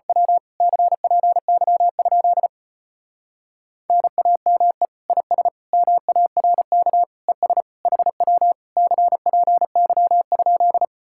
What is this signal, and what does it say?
Morse code, heard as a keyed tone, about 31 wpm, 700 Hz. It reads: W CPY? NAME IS MARK ES HW CPY?